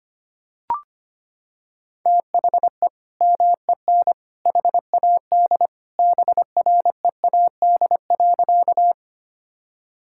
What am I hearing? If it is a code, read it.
Morse code, heard as a keyed tone, about 25 wpm, 700 Hz. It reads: THE MEN HAD BREAD.
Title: Sentences from Top 1600 Words 25wpm